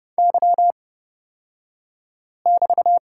Morse code, heard as a keyed tone, about 30 wpm, spaced as for 5 wpm, 700 Hz. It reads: Y<BT>
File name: Lesson 36d - BT - Two Characters 30wpm